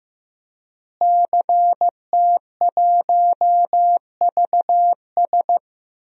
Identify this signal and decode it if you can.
Morse code, heard as a keyed tone, about 16 wpm, 700 Hz. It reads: CT1VS
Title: DXCC-100 Call Signs - 5 Characters - Repeated 2x - 15wpm